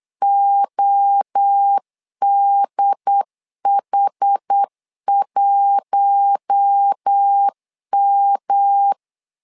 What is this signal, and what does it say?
Morse code, heard as a keyed tone, about 8 wpm, 800 Hz. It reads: ODH1M
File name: LEZIONI / 4 Q1 8wpm 3035